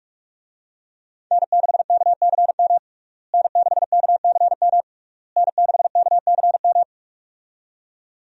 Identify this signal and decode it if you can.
Morse code, heard as a keyed tone, about 46 wpm, 700 Hz. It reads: N6KCM N6KCM N6KCM
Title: USA Call Signs - 5 Characters - Repeated 3x - 45wpm